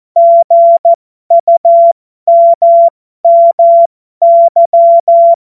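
Morse code, heard as a keyed tone, about 14 wpm, spaced as for 12 wpm, 680 Hz. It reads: GUMMY
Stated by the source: Morse code joke.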